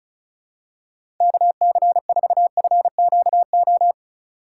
Morse code, heard as a keyed tone, about 36 wpm, 700 Hz. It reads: KC4FQO
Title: USA Call Signs - 6 Characters - 3x Character Spacing - 35wpm